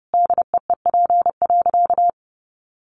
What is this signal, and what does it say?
Morse code, heard as a keyed tone, about 30 wpm, 700 Hz. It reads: DEEP.
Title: Sentences from Top 1300 Words 30wpm